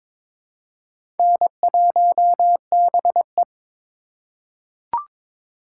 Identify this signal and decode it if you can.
Morse code, heard as a keyed tone, about 22 wpm, 700 Hz. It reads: N1BE
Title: USA Call Signs - 4 Characters - 6x Character Spacing - 22wpm